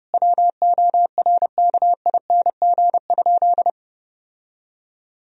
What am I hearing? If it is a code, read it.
Morse code, heard as a keyed tone, about 30 wpm, 700 Hz. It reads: WORKING?
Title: Sentences from Top 1000 Words - Review Part 4 - 30wpm